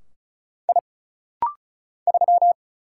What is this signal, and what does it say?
Morse code, heard as a keyed tone, about 33 wpm, 700 Hz. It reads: I 3
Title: Single Letter-Number Alt - Rapid-Fire 35wpm